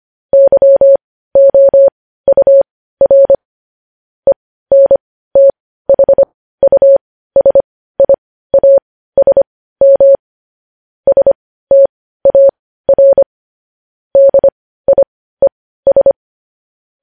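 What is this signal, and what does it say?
Morse code, heard as a keyed tone, about 25 wpm, spaced as for 15 wpm, 550 Hz. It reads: YOUR ENTHUSIASM STAR DIES
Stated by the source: Morse code news headlines at 15.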